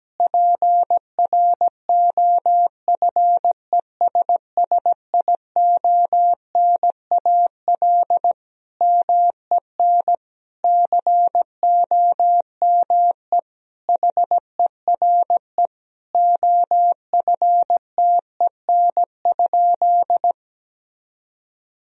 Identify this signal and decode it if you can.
Morse code, heard as a keyed tone, about 17 wpm, 700 Hz. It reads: PROFESSIONAL MEN COME HERE OFTEN?
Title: Sentences from Top 1000 Words - Review Part 4 - 17wpm